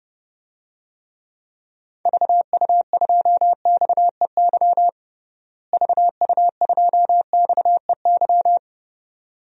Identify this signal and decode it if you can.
Morse code, heard as a keyed tone, about 30 wpm, 700 Hz. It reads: VU2XEY VU2XEY
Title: DXCC-100 Call Signs - 6 Characters - Repeated 2x - 30wpm